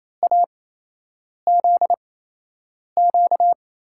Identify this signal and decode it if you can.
Morse code, heard as a keyed tone, about 28 wpm, spaced as for 8 wpm, 700 Hz. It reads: AZQ
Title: USA Call Signs - 8x Character Spacing - 28wpm